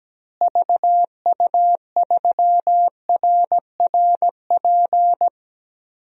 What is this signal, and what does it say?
Morse code, heard as a keyed tone, about 17 wpm, 700 Hz. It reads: VU3RRP
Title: DXCC-100 Call Signs - 6 Characters - Repeated 2x - 17wpm